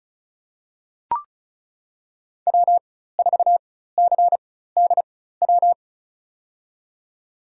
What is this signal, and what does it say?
Morse code, heard as a keyed tone, about 34 wpm, spaced as for 16 wpm, 700 Hz. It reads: W4CDW
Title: USA Call Signs - 5 Characters - 4x Character Spacing - 35wpm